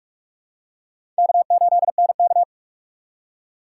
Morse code, heard as a keyed tone, about 44 wpm, 700 Hz. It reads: K8NK
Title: USA Call Signs - 4 Characters - 4x Character Spacing - 45wpm